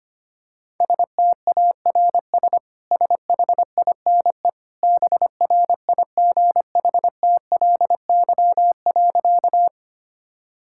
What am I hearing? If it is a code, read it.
Morse code, heard as a keyed tone, about 25 wpm, 700 Hz. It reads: STARS SHINE BRIGHTLY.
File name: Easy Sentences with Hints - 25wpm